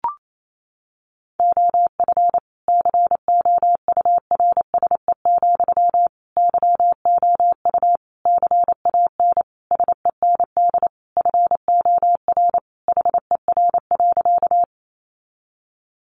Morse code, heard as a keyed tone, about 28 wpm, 700 Hz. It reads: OF COURSE, YOU CAN SEND FOR HER.
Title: Sentences from Top 1500 Words 28wpm